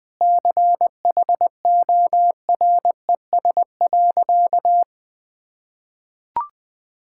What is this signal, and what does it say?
Morse code, heard as a keyed tone, about 20 wpm, 700 Hz. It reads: CHORES.
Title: Sentences with 5-Word Phrases - 20wpm